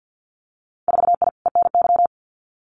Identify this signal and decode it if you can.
Morse code, heard as a keyed tone, about 50 wpm, 700 Hz. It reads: 4I ENY
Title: Winter Field Day Exchanges Encore 50wpm